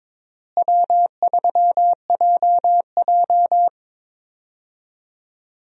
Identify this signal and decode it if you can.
Morse code, heard as a keyed tone, about 22 wpm, 700 Hz. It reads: W3JJ
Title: USA Call Signs - 4 Characters - Repeated 3x - 22wpm